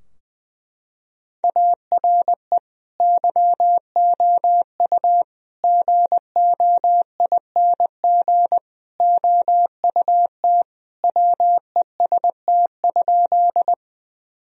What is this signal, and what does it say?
Morse code, heard as a keyed tone, about 20 wpm, 700 Hz. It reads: ARE YOU GOING OUT WEST?
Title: Sentences from Top 1000 Words - Review Part 2 - 20wpm